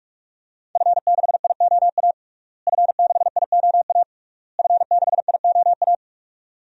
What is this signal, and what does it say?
Morse code, heard as a keyed tone, about 46 wpm, 700 Hz. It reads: F6IOA F6IOA F6IOA
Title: DXCC-100 Call Signs - 5 Characters - Repeated 3x - 45wpm